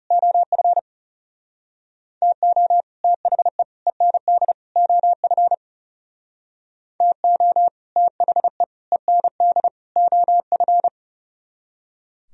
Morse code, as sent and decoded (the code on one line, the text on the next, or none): --- ..-. / - --- / - .... / . -. -.. / --- ..-. / - --- / - .... / . -. -.. / --- ..-.
OF TO THE END OF TO THE END OF